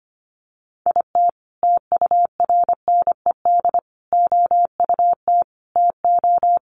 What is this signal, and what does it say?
Morse code, heard as a keyed tone, about 25 wpm, 700 Hz. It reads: IT TURNED OUT TO
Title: Sets of 5 Words 25wpm